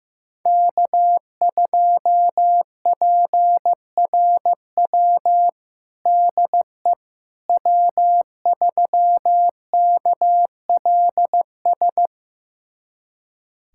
Morse code, heard as a keyed tone, about 16 wpm, 700 Hz. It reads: K2PRW DE W3KLS
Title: USA Callsign DE Callsign - 5 Characters - 15wpm